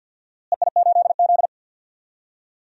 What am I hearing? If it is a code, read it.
Morse code, heard as a keyed tone, about 50 wpm, 700 Hz. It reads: EI8Z
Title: DXCC-100 Call Signs - 4 Characters - Repeated 3x - 50wpm